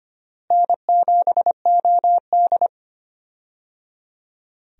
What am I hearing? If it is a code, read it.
Morse code, heard as a keyed tone, about 25 wpm, 700 Hz. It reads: N7OD
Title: USA Call Signs - 4 Characters - Repeated 3x - 25wpm